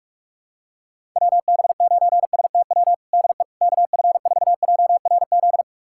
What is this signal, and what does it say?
Morse code, heard as a keyed tone, about 46 wpm, 700 Hz. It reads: WB9STW DE KF4JRZ
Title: USA Callsign DE Callsign 45wpm